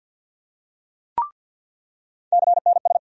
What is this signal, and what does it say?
Morse code, heard as a keyed tone, about 47 wpm, 700 Hz. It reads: CNS
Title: USA Call Sign Postfixes - 3 Letters 50wpm